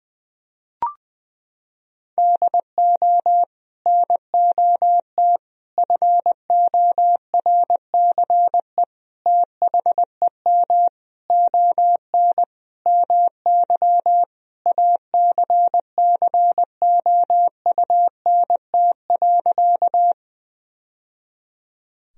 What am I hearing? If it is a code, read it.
Morse code, heard as a keyed tone, about 19 wpm, 700 Hz. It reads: DO NOT FORCE THEM ON MY ACCOUNT.
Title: Sentences from Top 1000 Words - Review Part 3 - 20wpm